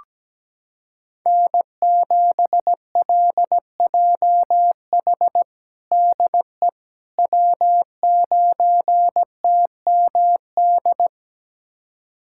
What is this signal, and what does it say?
Morse code, heard as a keyed tone, about 17 wpm, 700 Hz. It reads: N7LJH DE W9TMD
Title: USA Callsign DE Callsign - 5 Characters - 17wpm